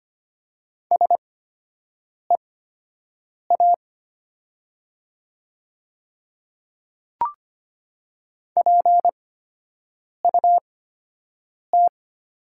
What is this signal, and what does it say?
Morse code, heard as a keyed tone, about 25 wpm, spaced as for 7 wpm, 700 Hz. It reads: SEA PUT
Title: 8x Character Spacing - Top 300 Words - 25wpm